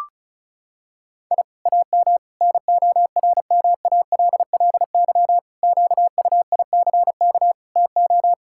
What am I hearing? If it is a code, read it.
Morse code, heard as a keyed tone, about 36 wpm, 700 Hz. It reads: I AM NORMALLY QUICK TO
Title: Sentences from Top 1000 Words - Review Part 6 - 35wpm